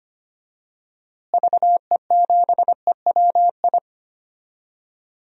Morse code, heard as a keyed tone, about 25 wpm, 700 Hz. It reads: VE7EWI
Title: DXCC-100 Call Signs - 6 Characters - Repeated 2x - 25wpm